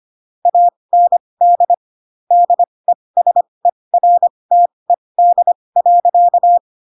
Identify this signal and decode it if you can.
Morse code, heard as a keyed tone, about 25 wpm, spaced as for 20 wpm, 700 Hz. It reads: AND DESERTED.
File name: Pinocchio / Pinocchio ch6 15 25 ik1hge